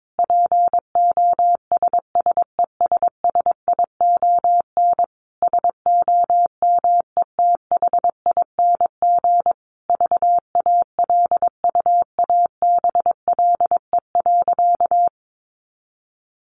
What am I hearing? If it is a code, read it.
Morse code, heard as a keyed tone, about 22 wpm, 700 Hz. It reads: POSSESSION SOMETHING VALUABLE.
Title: Sentences from Top 1400 Words 22wpm